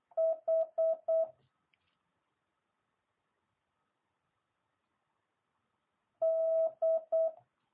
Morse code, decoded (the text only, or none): H D